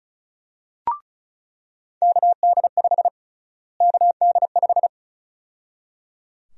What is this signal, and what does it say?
Morse code, heard as a keyed tone, about 34 wpm, 700 Hz. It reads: KD5 KD5
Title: USA Call Sign Prefixes - Repeated 2x - 35wpm